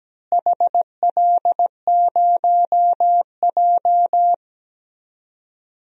Morse code, heard as a keyed tone, about 17 wpm, 700 Hz. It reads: HL0J
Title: DXCC-100 Call Signs - 4 Characters - Repeated 3x - 17wpm